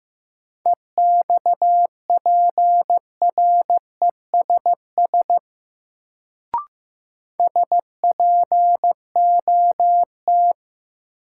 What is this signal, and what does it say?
Morse code, heard as a keyed tone, about 15 wpm, 700 Hz. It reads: EXPRESS SPOT
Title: Sentences from Top 1100 Words - New Word Drill Alt - 15wpm